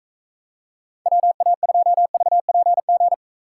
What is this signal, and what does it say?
Morse code, heard as a keyed tone, about 45 wpm, 700 Hz. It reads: WA2VPG